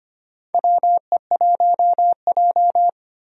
Morse code, heard as a keyed tone, about 25 wpm, 700 Hz. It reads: WE1J
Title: USA Call Signs - 4 Characters - 25wpm